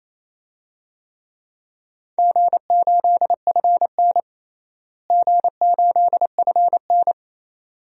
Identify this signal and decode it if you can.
Morse code, heard as a keyed tone, about 28 wpm, 700 Hz. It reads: G8FN G8FN
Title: DXCC-100 Call Signs - 4 Characters - Repeated 3x - 28wpm